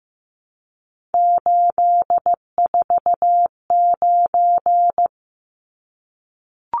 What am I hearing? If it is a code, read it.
Morse code, heard as a keyed tone, about 15 wpm, 700 Hz. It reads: 849
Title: Numbers - 3-Digit 15wpm